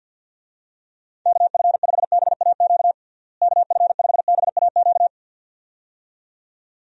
Morse code, heard as a keyed tone, about 50 wpm, 700 Hz. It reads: KF5BAQ KF5BAQ
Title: USA Call Signs - 6 Characters - Repeated 2x - 50wpm